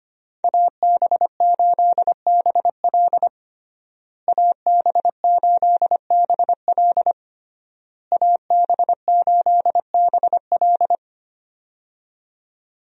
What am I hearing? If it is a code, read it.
Morse code, heard as a keyed tone, about 25 wpm, 700 Hz. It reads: AB8BL AB8BL AB8BL
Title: USA Call Signs - 5 Characters - Repeated 3x - 25wpm